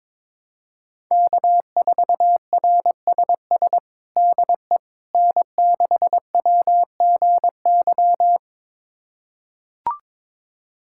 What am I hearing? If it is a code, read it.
Morse code, heard as a keyed tone, about 22 wpm, 700 Hz. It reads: K4RSS DE N6WGY
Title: USA Callsign DE Callsign 22wpm